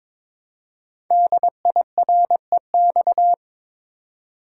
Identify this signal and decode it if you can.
Morse code, heard as a keyed tone, about 22 wpm, 700 Hz. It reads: DIREX
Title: Code Groups - 5 Letters - 22wpm